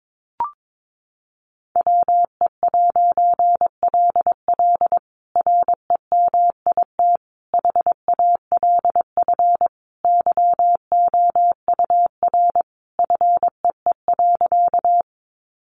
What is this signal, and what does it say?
Morse code, heard as a keyed tone, about 22 wpm, 700 Hz. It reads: WE'LL REMIT HALF YOUR FEE.
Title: A Sound Of Thunder 22wpm